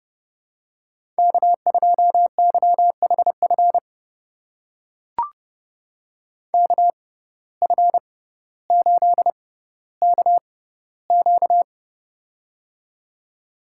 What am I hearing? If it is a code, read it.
Morse code, heard as a keyed tone, about 30 wpm, spaced as for 15 wpm, 700 Hz. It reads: K2YHF KF8KQ
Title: USA Call Signs - 5 Characters - 6x Character Spacing - 30wpm